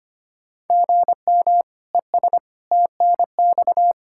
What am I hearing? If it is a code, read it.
Morse code, heard as a keyed tone, about 25 wpm, 700 Hz. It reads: GM ES TNX
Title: K1USN SST QSO Practice 25wpm